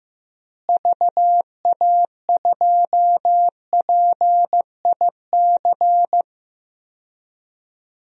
Morse code, heard as a keyed tone, about 16 wpm, 700 Hz. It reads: VA2PIC
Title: DXCC-100 Call Signs - 6 Characters - Repeated 3x - 15wpm